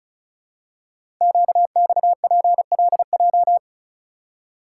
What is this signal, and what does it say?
Morse code, heard as a keyed tone, about 36 wpm, 690 Hz. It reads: QXPLJ
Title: Code Groups - 5 Letters - 35wpm